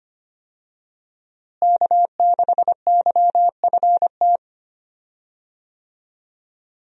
Morse code, heard as a keyed tone, about 25 wpm, 700 Hz. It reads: K6YFT
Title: USA Call Signs - 5 Characters - Repeated 3x - 25wpm